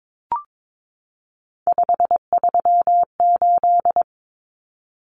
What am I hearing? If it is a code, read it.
Morse code, heard as a keyed tone, about 22 wpm, 700 Hz. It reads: 538
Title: Common Cut Numbers - 3-Digit 22wpm